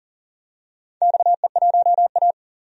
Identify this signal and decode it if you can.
Morse code, heard as a keyed tone, about 40 wpm, 700 Hz. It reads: XE1A
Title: DXCC-100 Call Signs - 4 Characters - 40wpm